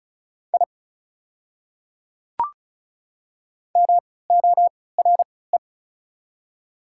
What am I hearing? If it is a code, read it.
Morse code, heard as a keyed tone, about 36 wpm, spaced as for 20 wpm, 700 Hz. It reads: I MORE
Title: 3x Character Spacing - Top 100 Words - 35wpm